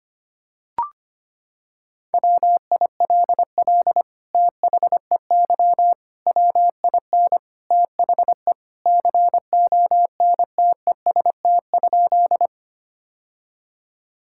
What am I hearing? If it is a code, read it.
Morse code, heard as a keyed tone, about 25 wpm, 700 Hz. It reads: WILL THEY WIN THE CONTEST?
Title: Sentences from Top 1600 Words 25wpm